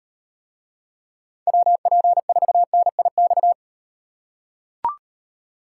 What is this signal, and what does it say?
Morse code, heard as a keyed tone, about 38 wpm, 700 Hz. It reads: WP4NIX